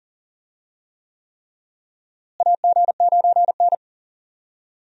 Morse code, heard as a keyed tone, about 37 wpm, 700 Hz. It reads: AG9N
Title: USA Call Signs - 4 Characters - Repeated 3x - 40wpm